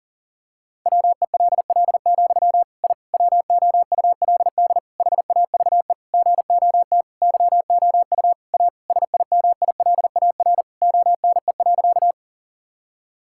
Morse code, heard as a keyed tone, about 40 wpm, 700 Hz. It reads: WELL, I WOULD HAVE GOT YOU A SIMILAR ONE.
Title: Sentences from Top 1000 Words - Review Part 5 - 40wpm